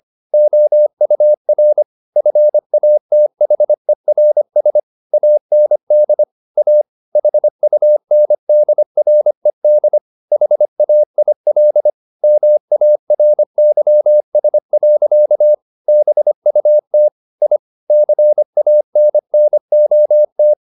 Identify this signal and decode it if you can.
Morse code, heard as a keyed tone, about 25 wpm, 600 Hz. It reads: OUR FATHERS AND A HUNDRED HAIL MARYS. BUT I CANNOT